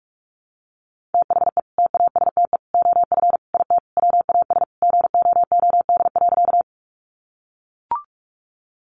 Morse code, heard as a keyed tone, about 44 wpm, 700 Hz. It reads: THE TASTE OF IT WAS GOOD.
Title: Sentences from Top 1100 Words 45wpm